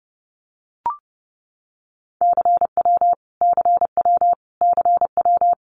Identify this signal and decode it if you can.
Morse code, heard as a keyed tone, about 30 wpm, 700 Hz. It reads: CW CW CW
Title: Abbreviations Repeated 3x - 30wpm